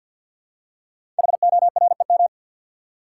Morse code, heard as a keyed tone, about 52 wpm, 700 Hz. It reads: HOREM
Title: Code Groups - 5 Letters - 50wpm